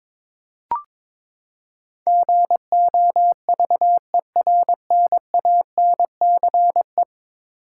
Morse code, heard as a keyed tone, about 22 wpm, 700 Hz. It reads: GOVERNANCE
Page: morse-code-ninja-practice-22wpm